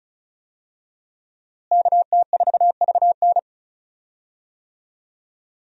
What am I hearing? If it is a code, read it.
Morse code, heard as a keyed tone, about 36 wpm, 700 Hz. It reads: KT4VN
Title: USA Call Signs - 5 Characters - Repeated 2x - 35wpm